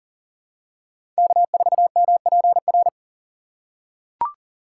Morse code, heard as a keyed tone, about 40 wpm, 700 Hz. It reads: K4MPR
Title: USA Call Signs - 5 Characters - 6x Character Spacing - 40wpm